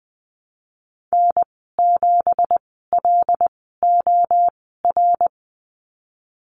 Morse code, heard as a keyed tone, about 20 wpm, spaced as for 15 wpm, 700 Hz. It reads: N7LOR
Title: 179 Lesson 29g - Slash - Call Signs 20wpm